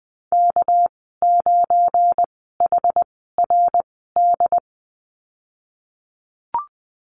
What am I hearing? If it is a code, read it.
Morse code, heard as a keyed tone, about 20 wpm, spaced as for 15 wpm, 700 Hz. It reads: K9HRD
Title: 200 Lesson 32g - Letter k - Call Signs 20wpm